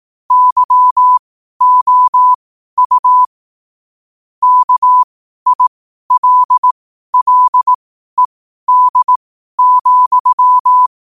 Morse code, heard as a keyed tone, about 18 wpm, spaced as for 12 wpm, 1000 Hz. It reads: YOU KILLED,
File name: qotdfiles / 12wpm